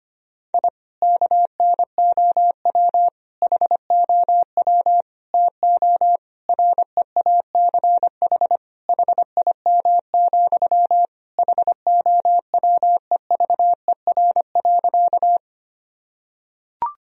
Morse code, heard as a keyed tone, about 25 wpm, 700 Hz. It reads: I KNOW HOW TO REACH HIM, HOWEVER.
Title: Sentences from Top 1000 Words - Review Part 2 - 25wpm